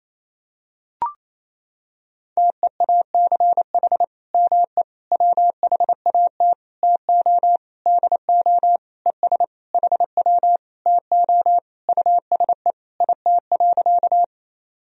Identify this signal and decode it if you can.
Morse code, heard as a keyed tone, about 28 wpm, 700 Hz. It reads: TEACH ME WHAT TO DO ES HW TO USE IT.
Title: Sentences from Top 500 Words and Abbreviations 28wpm